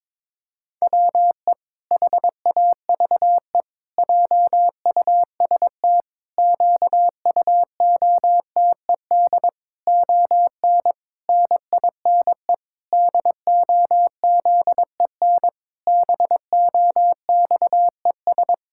Morse code, heard as a keyed tone, about 22 wpm, 700 Hz. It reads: WE HAVE JUST QUOTED ON NINE DOZEN BOXES
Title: Pangram Challenge v2 - 22wpm